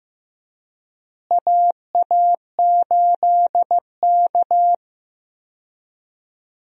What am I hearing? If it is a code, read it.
Morse code, heard as a keyed tone, about 15 wpm, 700 Hz. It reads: AA8K
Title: USA Call Signs - 4 Characters - Repeated 3x - 15wpm